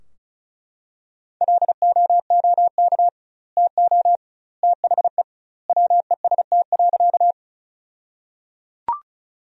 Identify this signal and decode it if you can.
Morse code, heard as a keyed tone, about 34 wpm, 700 Hz. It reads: LOOK TO THE WEST.
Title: 2x Word-Spacing - Sentences from Top 400 Words - 35wpm